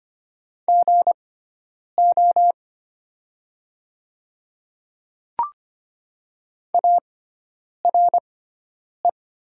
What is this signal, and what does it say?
Morse code, heard as a keyed tone, about 25 wpm, spaced as for 9 wpm, 700 Hz. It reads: GO ARE
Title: 6x Character Spacing - Top 100 Words - 25wpm